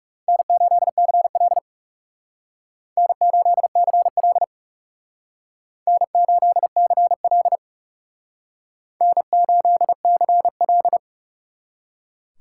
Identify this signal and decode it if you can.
Morse code, heard as a keyed tone, about 38 wpm, 700 Hz. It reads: N8CL N8CL N8CL N8CL